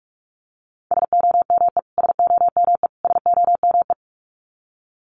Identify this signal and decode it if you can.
Morse code, heard as a keyed tone, about 44 wpm, 700 Hz. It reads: SOME SOME SOME
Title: Top 500 Words Repeated 45wpm